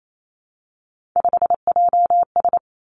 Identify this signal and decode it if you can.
Morse code, heard as a keyed tone, about 28 wpm, 700 Hz. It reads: 5JS
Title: Three Challenging Letter-Number Combinations 28wpm